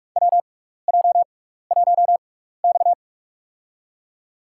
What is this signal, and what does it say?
Morse code, heard as a keyed tone, about 44 wpm, spaced as for 15 wpm, 700 Hz. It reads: WJ1X
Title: USA Call Signs - 6x Character Spacing - 45wpm